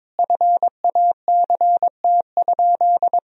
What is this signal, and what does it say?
Morse code, heard as a keyed tone, about 23 wpm, 700 Hz. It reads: FACT?
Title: Sentences from Top 1000 Words - Review Part 3 - 22wpm